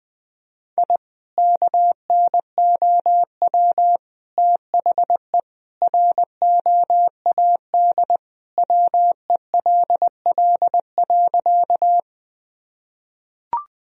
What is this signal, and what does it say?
Morse code, heard as a keyed tone, about 21 wpm, 700 Hz. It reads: I KNOW THE ROAD WELL.
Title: Sentences from Top 1000 Words - Review Part 2 - 20wpm